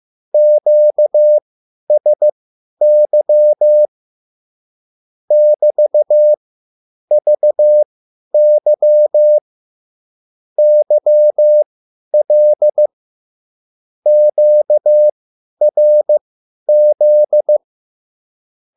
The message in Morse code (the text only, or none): QSY <BT>VY YL QRZ